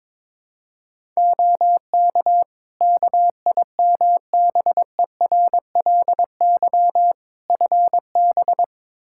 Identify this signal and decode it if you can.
Morse code, heard as a keyed tone, about 21 wpm, 700 Hz. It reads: OK KIMBERLY FB
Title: Practice QSO - Exchange 2 - Info - 22wpm